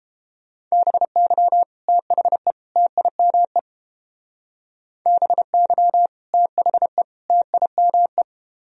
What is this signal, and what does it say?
Morse code, heard as a keyed tone, about 31 wpm, 700 Hz. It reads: BY THE TIME BY THE TIME